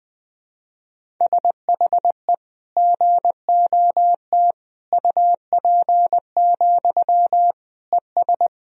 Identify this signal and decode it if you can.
Morse code, heard as a keyed tone, about 20 wpm, 700 Hz. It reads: SHE GOT UP, ES
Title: Sentences from Top 500 Words and Abbreviations 20wpm